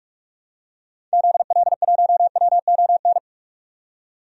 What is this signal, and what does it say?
Morse code, heard as a keyed tone, about 46 wpm, 700 Hz. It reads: ZL1WON
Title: DXCC-100 Call Signs - 6 Characters - 45wpm